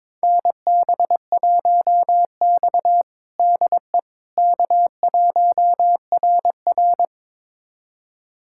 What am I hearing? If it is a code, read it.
Morse code, heard as a keyed tone, about 22 wpm, 700 Hz. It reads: NB1X DE K1RR
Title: USA Callsign DE Callsign - 4 Characters - 22wpm